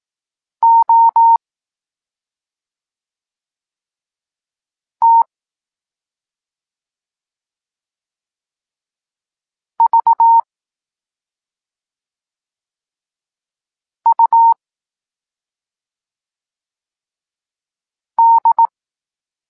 Morse code, heard as a keyed tone, about 18 wpm, 920 Hz. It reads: O T V U D
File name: Tekst-23 4